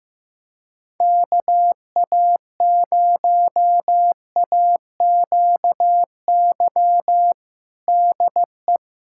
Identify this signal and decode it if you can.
Morse code, heard as a keyed tone, about 15 wpm, 700 Hz. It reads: KA0AQY DE
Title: USA Callsign DE Callsign - 6 Characters - 15wpm